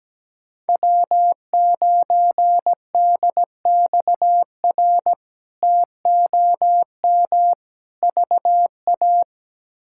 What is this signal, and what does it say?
Morse code, heard as a keyed tone, about 17 wpm, 700 Hz. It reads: W9DXR TOM VA
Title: K1USN SST QSO Practice - 17wpm